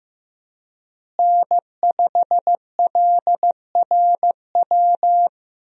Morse code, heard as a keyed tone, about 15 wpm, 700 Hz. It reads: N5LRW
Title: USA Call Signs - 5 Characters - Repeated 2x - 15wpm